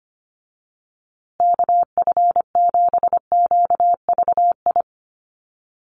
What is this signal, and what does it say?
Morse code, heard as a keyed tone, about 25 wpm, 700 Hz. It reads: KF7QVI
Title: USA Call Signs Repeated 3x 25wpm